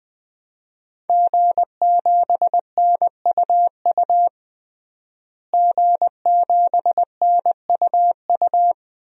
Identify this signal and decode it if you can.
Morse code, heard as a keyed tone, about 20 wpm, 700 Hz. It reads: G7NUU G7NUU
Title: DXCC-100 Call Signs - 5 Characters - Repeated 2x - 20wpm